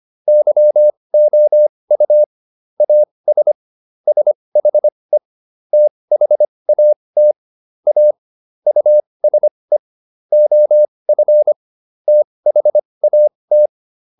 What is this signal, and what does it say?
Morse code, heard as a keyed tone, about 25 wpm, spaced as for 20 wpm, 600 Hz. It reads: YOU AS SHE THAT A USE OF THAT